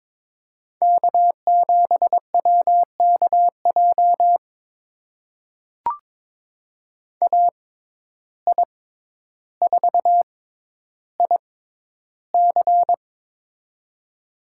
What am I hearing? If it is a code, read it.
Morse code, heard as a keyed tone, about 23 wpm, spaced as for 11 wpm, 700 Hz. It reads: K7WKJ AI4IC
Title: USA Call Signs - 5 Characters - 6x Character Spacing - 22wpm